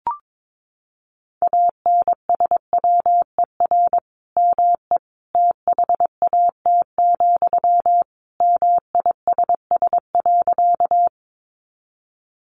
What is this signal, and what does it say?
Morse code, heard as a keyed tone, about 22 wpm, 700 Hz. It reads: ANSWER ME THAT, MISS.
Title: Sentences from Top 500 Words 22wpm